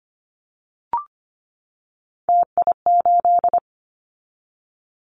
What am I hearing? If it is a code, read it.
Morse code, heard as a keyed tone, about 25 wpm, 700 Hz. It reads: TI8
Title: Callsign Prefixes for Top 100 DXCC - 25wpm